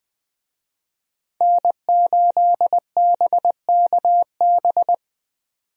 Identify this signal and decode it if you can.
Morse code, heard as a keyed tone, about 20 wpm, 700 Hz. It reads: N8BKB
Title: USA Call Signs - 5 Characters - 20wpm